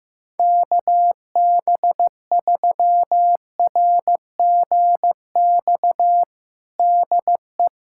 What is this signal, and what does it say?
Morse code, heard as a keyed tone, about 15 wpm, 700 Hz. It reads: KB3RGX DE
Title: Practice QSO - Exchange 4 - Ending - 15wpm